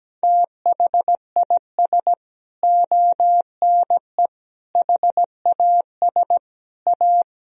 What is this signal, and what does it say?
Morse code, heard as a keyed tone, about 17 wpm, 700 Hz. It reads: THIS ONE HAS A
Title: Sentences from Top 1000 Words - Review Part 3 - 17wpm